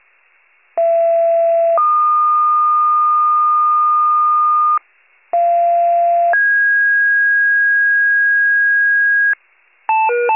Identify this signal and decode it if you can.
Two-tone page, heard as a one-second tone, then a three-second tone, about 670 then 1220 Hz, about 690 then 1650 Hz, with noise. It is heard from 0.8 to 9.3 s.